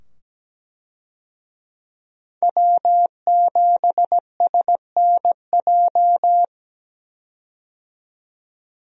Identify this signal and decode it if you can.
Morse code, heard as a keyed tone, about 17 wpm, 700 Hz. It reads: W7SNJ
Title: USA Call Signs - 5 Characters - 17wpm